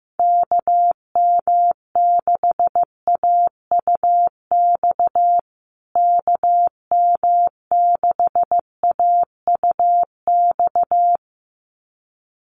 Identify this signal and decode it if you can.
Morse code, heard as a keyed tone, about 15 wpm, 700 Hz. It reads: KM6AUX KM6AUX
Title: USA Call Signs Repeated 2x 15wpm